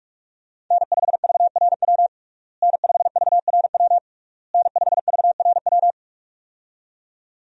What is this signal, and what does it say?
Morse code, heard as a keyed tone, about 46 wpm, 700 Hz. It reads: N5VRW N5VRW N5VRW
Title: USA Call Signs - 5 Characters - Repeated 3x - 45wpm